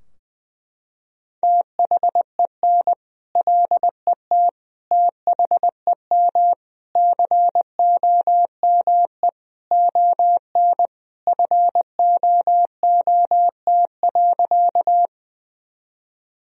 Morse code, heard as a keyed tone, about 20 wpm, 700 Hz. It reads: THEN LET THEM COME ON FOOT.
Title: Sentences from Top 1000 Words - Review Part 3 - 20wpm